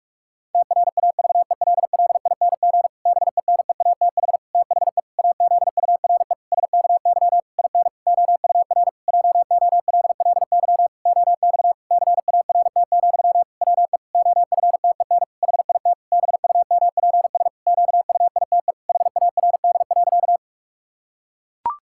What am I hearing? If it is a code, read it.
Morse code, heard as a keyed tone, about 46 wpm, 700 Hz. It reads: TRAVELLING BENEATH THE AZURE SKY IN OUR JOLLY OX CART, WE OFTEN HIT BUMPS QUITE HARD.
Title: Pangram Challenge v2 - 45wpm